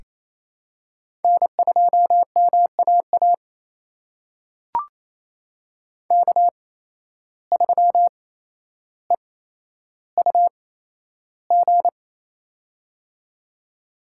Morse code, heard as a keyed tone, about 28 wpm, spaced as for 12 wpm, 700 Hz. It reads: N2MAA K3EUG
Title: USA Call Signs - 5 Characters - 8x Character Spacing - 28wpm